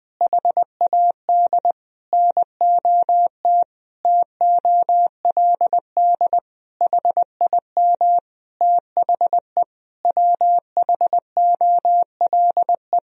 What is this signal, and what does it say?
Morse code, heard as a keyed tone, about 20 wpm, 700 Hz. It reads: HAD NOT TOLD HIM THE WHOLE
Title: Sentences from Top 1000 Words - Review Part 1 - 20wpm